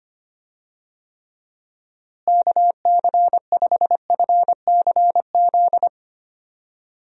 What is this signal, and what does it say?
Morse code, heard as a keyed tone, about 25 wpm, 700 Hz. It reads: KC5FCZ
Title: USA Call Signs - 6 Characters - Repeated 2x - 25wpm